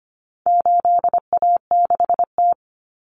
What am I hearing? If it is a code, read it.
Morse code, heard as a keyed tone, about 25 wpm, 700 Hz. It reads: 8A6T
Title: Common Cut Numbers - 4-Digit 25wpm